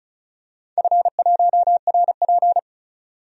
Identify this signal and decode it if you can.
Morse code, heard as a keyed tone, about 36 wpm, 700 Hz. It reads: F1RP
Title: DXCC-100 Call Signs - 4 Characters - 35wpm